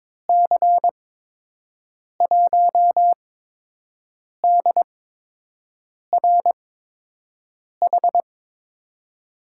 Morse code, heard as a keyed tone, about 22 wpm, spaced as for 6 wpm, 700 Hz. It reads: C1DRH
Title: USA Call Signs - 6 Characters - 8x Character Spacing - 22wpm